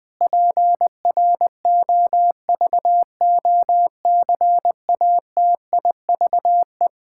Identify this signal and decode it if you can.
Morse code, heard as a keyed tone, about 20 wpm, 700 Hz. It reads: PROVOCATIVE
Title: Pangram Challenge v2 - 20wpm